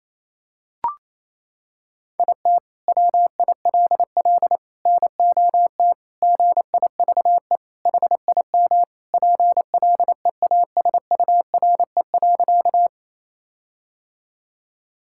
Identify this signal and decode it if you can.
Morse code, heard as a keyed tone, about 28 wpm, 700 Hz. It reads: IT WILL NOT GIVE HIM PLEASURE.
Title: Sentences from Top 1600 Words 28wpm